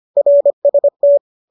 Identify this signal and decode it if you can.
Morse code, heard as a keyed tone, about 25 wpm, 570 Hz. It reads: RST
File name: qso221 25